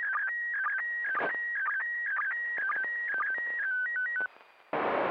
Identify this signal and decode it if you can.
SSTV (slow-scan television), PD 120.
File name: SSTV ISS